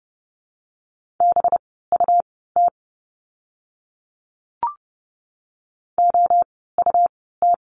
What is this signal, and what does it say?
Morse code, heard as a keyed tone, about 30 wpm, spaced as for 17 wpm, 700 Hz. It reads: BUT OUT
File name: course / 30:10wpm / 237 Lesson 38f - Letter q - Words 30wpm